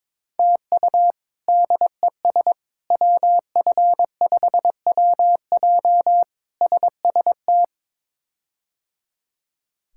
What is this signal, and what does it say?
Morse code, heard as a keyed tone, about 22 wpm, 700 Hz. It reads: TU DES WF5WJ SST
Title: K1USN SST QSO Practice - 22wpm